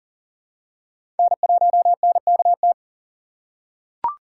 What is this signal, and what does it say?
Morse code, heard as a keyed tone, about 40 wpm, 700 Hz. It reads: N1NKT
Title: USA Call Signs - 6x Character Spacing - 40wpm